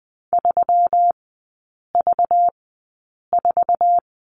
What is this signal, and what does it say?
Morse code, heard as a keyed tone, about 20 wpm, spaced as for 8 wpm, 700 Hz. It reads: 3V4
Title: 205 Lesson 33e - Letter j - Three Characters 20wpm